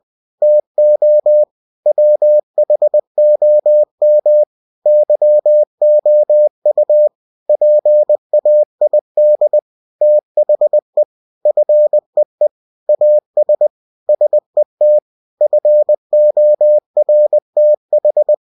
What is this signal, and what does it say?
Morse code, heard as a keyed tone, about 20 wpm, 600 Hz. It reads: TO WHOM YOU PAID THE FEE AS SET FORTH
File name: cw-books / winnie-the-pooh / winnie-20wpm-page-0113